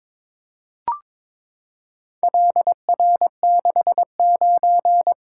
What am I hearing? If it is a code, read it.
Morse code, heard as a keyed tone, about 22 wpm, 700 Hz. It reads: LR69
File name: practice / 22wpm / Four Letter-Number Combinations - 22wpm